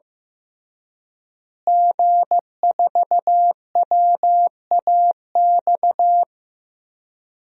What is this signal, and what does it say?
Morse code, heard as a keyed tone, about 15 wpm, 700 Hz. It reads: G4WAX
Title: DXCC-100 Call Signs - 5 Characters - Repeated 3x - 15wpm